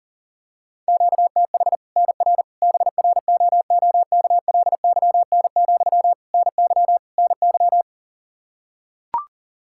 Morse code, heard as a keyed tone, about 37 wpm, 700 Hz. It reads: QTH NR BROOKLYN, NY NY
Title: Practice QSO - Exchange 1 - Beginning - 40wpm